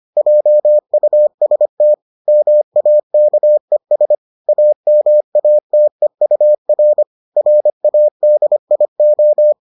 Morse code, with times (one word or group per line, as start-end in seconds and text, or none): 0.2-1.9 JUST
2.3-4.2 MAKES
4.5-7.0 AMATEUR
7.4-9.5 RADIO